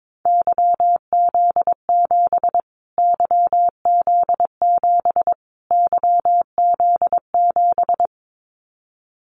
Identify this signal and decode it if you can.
Morse code, heard as a keyed tone, about 22 wpm, 700 Hz. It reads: YZ7 YZ7 YZ7
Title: Callsign Prefixes for Top 25 DXCC - Repeated 3x - 22wpm